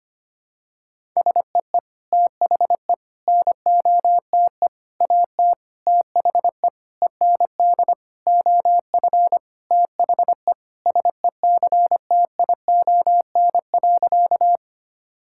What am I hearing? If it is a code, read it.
Morse code, heard as a keyed tone, about 25 wpm, 700 Hz. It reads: SEE THE NOTE AT THE END OF THE SECTION.
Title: Sentences from Top 1000 Words - Review Part 5 - 25wpm